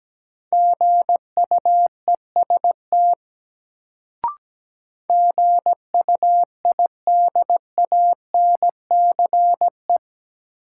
Morse code, heard as a keyed tone, about 17 wpm, 700 Hz. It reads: GUEST GUIDANCE
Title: Sentences from Top 1600 Words - New Word Drill - 17wpm